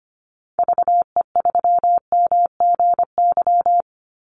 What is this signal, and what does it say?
Morse code, heard as a keyed tone, about 25 wpm, 700 Hz. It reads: VE3MGY
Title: Call Signs Repeated 3x - 25wpm